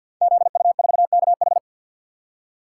Morse code, heard as a keyed tone, about 53 wpm, 700 Hz. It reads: ZU4KH
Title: Code Groups - 5 Letter:Numbers - 50wpm